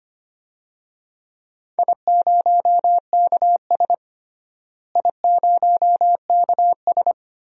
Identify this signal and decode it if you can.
Morse code, heard as a keyed tone, about 25 wpm, 700 Hz. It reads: I0KS I0KS
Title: DXCC-100 Call Signs - 4 Characters - Repeated 2x - 25wpm